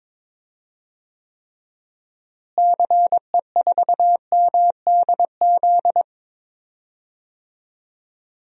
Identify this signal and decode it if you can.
Morse code, heard as a keyed tone, about 22 wpm, 700 Hz. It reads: CE4MDZ
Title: DXCC-100 Call Signs - 6 Characters - Repeated 3x - 22wpm